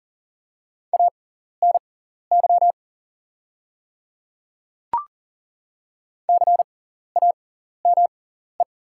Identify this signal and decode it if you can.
Morse code, heard as a keyed tone, about 40 wpm, spaced as for 14 wpm, 690 Hz. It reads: ANY CAME
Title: 6x Character Spacing - Top 200 Words - 40wpm